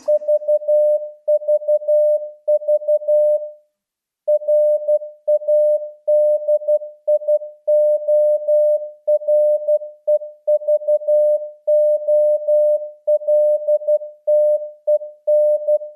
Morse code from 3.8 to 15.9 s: RADIOREVOLTEN